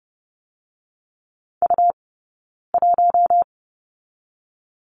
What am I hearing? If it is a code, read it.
Morse code, heard as a keyed tone, about 30 wpm, 700 Hz. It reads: U 1
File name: Lesson 17d - Letter u - Two Characters 30wpm